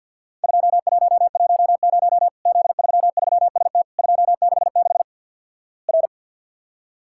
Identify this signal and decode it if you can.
Morse code, heard as a keyed tone, about 48 wpm, 700 Hz. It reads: 2110 Z33ST 266 R
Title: Worked All Europe DX Contest - QTC Exchange - 50wpm